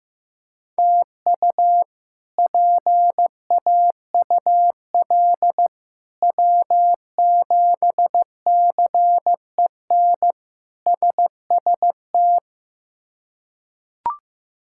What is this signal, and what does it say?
Morse code, heard as a keyed tone, about 15 wpm, 700 Hz. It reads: TU PAUL W7CEN SST
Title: K1USN SST QSO Practice - 15wpm